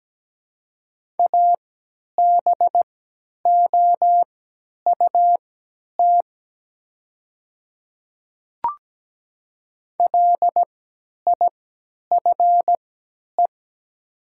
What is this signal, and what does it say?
Morse code, heard as a keyed tone, about 17 wpm, spaced as for 10 wpm, 700 Hz. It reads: ABOUT LIFE
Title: 3x Character Spacing - Top 500 Words - 17wpm